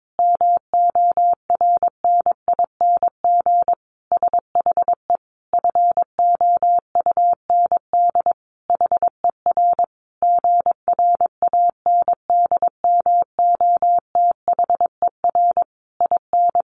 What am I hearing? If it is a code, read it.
Morse code, heard as a keyed tone, about 22 wpm, 700 Hz. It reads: MORNING SHE FOUND HER GRANDMOTHER IN
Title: Sentences from Top 1200 Words 22wpm